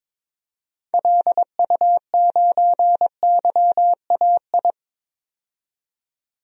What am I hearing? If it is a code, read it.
Morse code, heard as a keyed tone, about 21 wpm, 700 Hz. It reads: LU9YAI
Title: DXCC-100 Call Signs - 6 Characters - 22wpm